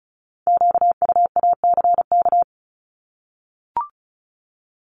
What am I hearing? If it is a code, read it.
Morse code, heard as a keyed tone, about 36 wpm, 700 Hz. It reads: QUACK
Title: Multi-letter Phonemes 35wpm